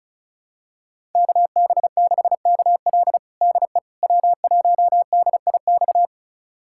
Morse code, heard as a keyed tone, about 36 wpm, 700 Hz. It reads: KB6KL DE W1DIX
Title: USA Callsign DE Callsign 35wpm